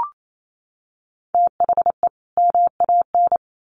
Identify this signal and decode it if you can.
Morse code, heard as a keyed tone, about 28 wpm, 700 Hz. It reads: THE MAN
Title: Sets of 2 Words 28wpm